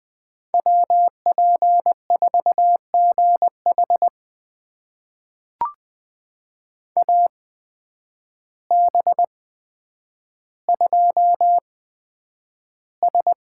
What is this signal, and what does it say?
Morse code, heard as a keyed tone, about 20 wpm, spaced as for 9 wpm, 700 Hz. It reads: WP4GH AB2S
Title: USA Call Signs - 5 Characters - 8x Character Spacing - 20wpm